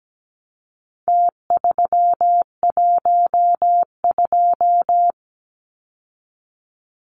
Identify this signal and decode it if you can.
Morse code, heard as a keyed tone, about 17 wpm, 700 Hz. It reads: T312
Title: Numbers - Contest Style - 4-Digit 17wpm